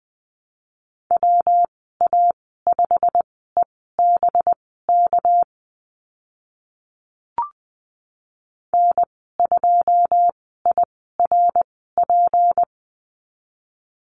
Morse code, heard as a keyed tone, about 20 wpm, spaced as for 14 wpm, 700 Hz. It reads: WA5EBK N2IRP
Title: 200 Lesson 32g - Letter k - Call Signs 20wpm